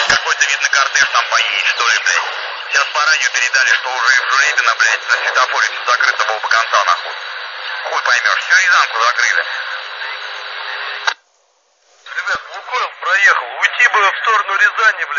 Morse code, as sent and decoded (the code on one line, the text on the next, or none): none